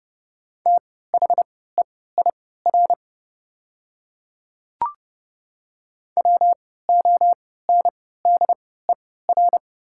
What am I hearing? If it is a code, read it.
Morse code, heard as a keyed tone, about 30 wpm, spaced as for 17 wpm, 690 Hz. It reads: THEIR WONDER
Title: 3x Character Spacing - Top 500 Words - 30wpm